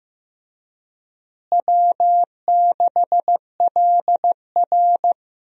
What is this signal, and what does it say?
Morse code, heard as a keyed tone, about 15 wpm, 700 Hz. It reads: W6LR
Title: USA Call Signs - 4 Characters - Repeated 3x - 15wpm